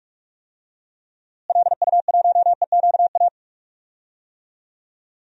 Morse code, heard as a keyed tone, about 46 wpm, 700 Hz. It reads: LU1EQA
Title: DXCC-100 Call Signs - 6 Characters - Repeated 3x - 45wpm